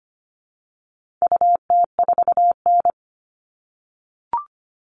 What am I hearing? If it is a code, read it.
Morse code, heard as a keyed tone, about 25 wpm, 700 Hz. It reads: UT4N
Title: All Cut Numbers - 4-Digit 25wpm